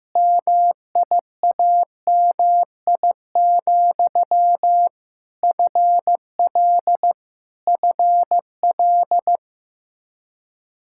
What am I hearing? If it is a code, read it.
Morse code, heard as a keyed tone, about 15 wpm, 700 Hz. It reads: MIAMI, FL FL
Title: Practice QSO - Exchange 1 - Beginning - 15wpm